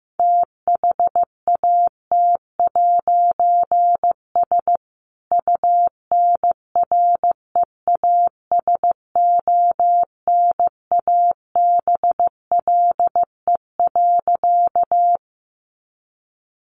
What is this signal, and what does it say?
Morse code, heard as a keyed tone, about 15 wpm, 700 Hz. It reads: THAT'S UNREASONABLE.
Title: A Sound Of Thunder 15wpm